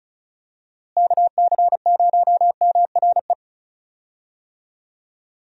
Morse code, heard as a keyed tone, about 34 wpm, 700 Hz. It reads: KC0MRE
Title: USA Call Signs - 6 Characters - Repeated 2x - 35wpm